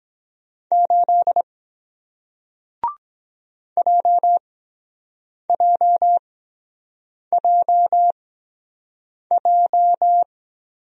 Morse code, heard as a keyed tone, about 23 wpm, spaced as for 7 wpm, 700 Hz. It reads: 8 JJJJ